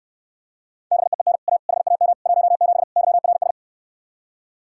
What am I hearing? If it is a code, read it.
Morse code, heard as a keyed tone, about 69 wpm, 700 Hz. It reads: BEEN A HAM 17 YRS